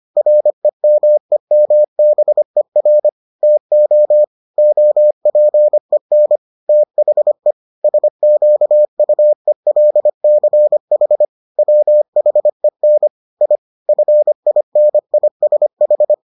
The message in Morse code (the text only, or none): REMEMBER TO OPEN THE SQUELCH WHEN I FINISH